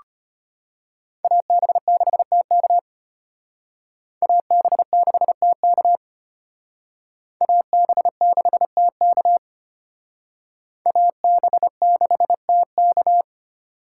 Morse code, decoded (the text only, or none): AB6TK AB6TK AB6TK AB6TK